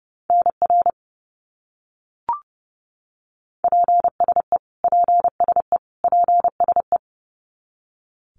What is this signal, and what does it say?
Morse code, heard as a keyed tone, about 30 wpm, 700 Hz. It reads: NR PSE PSE PSE
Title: Abbreviations Repeated 3x - 30wpm